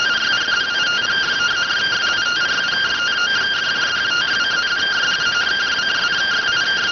Two-tone teleprinter signal, about 170 Hz shift.